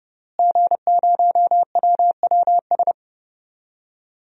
Morse code, heard as a keyed tone, about 32 wpm, 700 Hz. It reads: G0WWS
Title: DXCC-100 Call Signs - 5 Characters - Repeated 2x - 30wpm